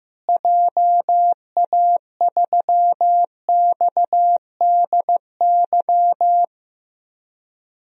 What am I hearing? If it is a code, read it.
Morse code, heard as a keyed tone, about 15 wpm, 700 Hz. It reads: JA3XDY
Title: DXCC-100 Call Signs - 6 Characters - Repeated 2x - 15wpm